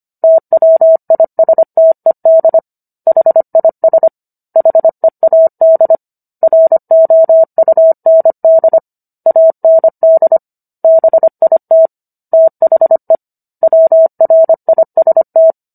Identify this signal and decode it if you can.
Morse code, heard as a keyed tone, about 25 wpm, 650 Hz. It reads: TWISTED HIS HEAD ROUND AND BIT THE WRIST